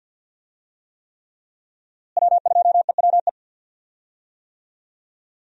Morse code, heard as a keyed tone, about 52 wpm, 700 Hz. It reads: W2EWE
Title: USA Call Signs - 5 Characters - Repeated 3x - 50wpm